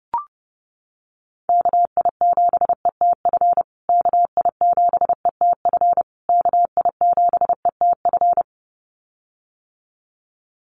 Morse code, heard as a keyed tone, about 30 wpm, 700 Hz. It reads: KI7ETF KI7ETF KI7ETF
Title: USA Call Signs Repeated 3x 30wpm